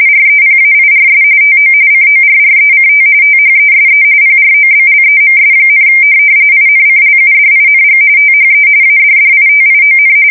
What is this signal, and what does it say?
RTTY, about 45.45 baud. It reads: REPAIRS TO CORRECT THESE PROBLEMS, INCLUDING THE ADDITION
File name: fsk170 / USS-Massachusetts